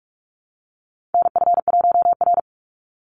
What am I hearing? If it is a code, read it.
Morse code, heard as a keyed tone, about 44 wpm, 700 Hz. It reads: NF1R
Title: NA SOTA Chaser Callsigns 45wpm